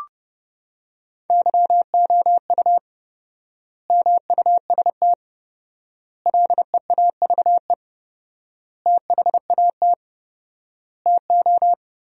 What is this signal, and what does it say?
Morse code, heard as a keyed tone, about 32 wpm, 700 Hz. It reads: YOU MUST LEAVE THAT TO
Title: 4x Word-Spacing - Sentences from Top 400 Words - 30wpm